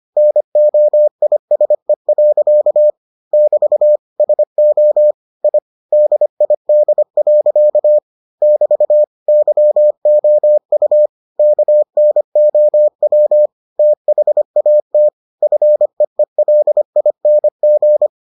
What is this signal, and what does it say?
Morse code, heard as a keyed tone, about 25 wpm, 600 Hz. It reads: NOISE. <BT> SO I DID. <BT> YOU KNOW THAT FEELING